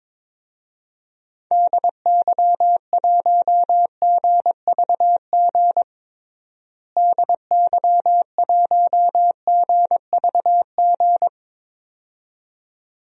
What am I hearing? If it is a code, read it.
Morse code, heard as a keyed tone, about 21 wpm, 700 Hz. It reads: DY1GVG DY1GVG
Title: DXCC-100 Call Signs - 6 Characters - Repeated 2x - 22wpm